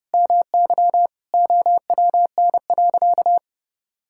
Morse code, heard as a keyed tone, about 32 wpm, 700 Hz. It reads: MY OWN.
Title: Sentences from Top 1000 Words - Review Part 6 - 30wpm